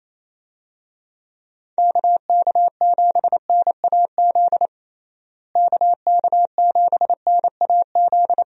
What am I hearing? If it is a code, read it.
Morse code, heard as a keyed tone, about 28 wpm, 700 Hz. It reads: KK7NAZ KK7NAZ
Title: USA Call Signs - 6 Characters - Repeated 3x - 28wpm